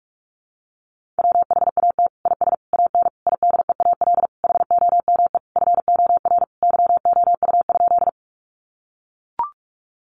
Morse code, heard as a keyed tone, about 44 wpm, 700 Hz. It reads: WHAT IS AN IDEAL HOME FOR YOU?
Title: Sentences from Top 1400 Words 45wpm